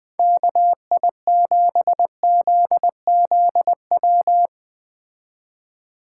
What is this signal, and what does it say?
Morse code, heard as a keyed tone, about 20 wpm, 700 Hz. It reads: KI7ZZW
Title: Speed Racing - USA Call Signs 17wpm